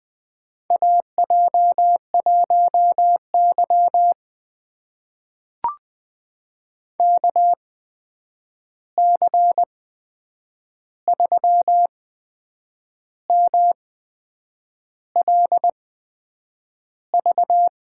Morse code, heard as a keyed tone, about 20 wpm, spaced as for 7 wpm, 700 Hz. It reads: AJ1Y KC3MLV
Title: USA Call Signs - 8x Character Spacing - 20wpm